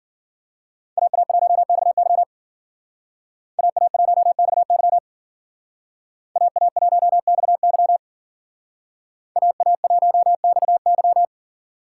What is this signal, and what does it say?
Morse code, heard as a keyed tone, about 52 wpm, 700 Hz. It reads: AA1XY AA1XY AA1XY AA1XY